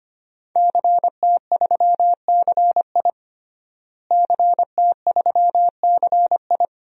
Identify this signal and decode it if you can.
Morse code, heard as a keyed tone, about 25 wpm, 700 Hz. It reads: CT3CI CT3CI
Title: DXCC-100 Call Signs - 5 Characters - Repeated 3x - 25wpm